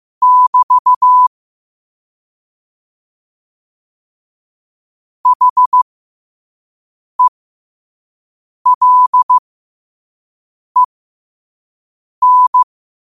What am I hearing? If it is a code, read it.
Morse code, heard as a keyed tone, about 15 wpm, spaced as for 5 wpm, 1000 Hz. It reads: <BT> HELEN
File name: qotdfiles / 5wpm